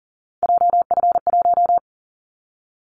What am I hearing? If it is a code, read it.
Morse code, heard as a keyed tone, about 42 wpm, 700 Hz. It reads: JF1
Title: Callsign Prefixes for Top 25 DXCC - 40wpm